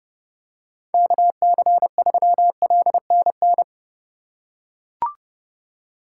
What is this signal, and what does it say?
Morse code, heard as a keyed tone, about 30 wpm, 700 Hz. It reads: KC3LNN
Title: USA Call Signs - 6 Characters - 2x Character Spacing - 30wpm